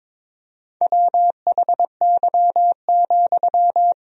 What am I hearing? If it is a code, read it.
Morse code, heard as a keyed tone, about 22 wpm, 700 Hz. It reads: WHY,
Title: 4x Word-Spacing - Sentences from Top 200 Words - 22wpm